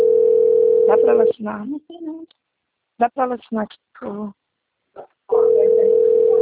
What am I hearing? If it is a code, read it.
Ringback tone (North America), 440 + 480 Hz, with speech.
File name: rg-1600-unknown- - - .15945